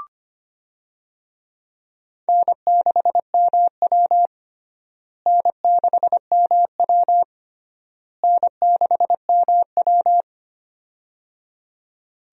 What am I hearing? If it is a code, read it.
Morse code, heard as a keyed tone, about 25 wpm, 700 Hz. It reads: N6MW N6MW N6MW
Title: USA Call Signs - 4 Characters - Repeated 3x - 25wpm